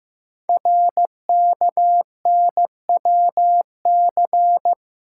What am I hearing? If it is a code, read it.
Morse code, heard as a keyed tone, about 15 wpm, 700 Hz. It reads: RKNWC
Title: Code Groups - 5 Letters - 15wpm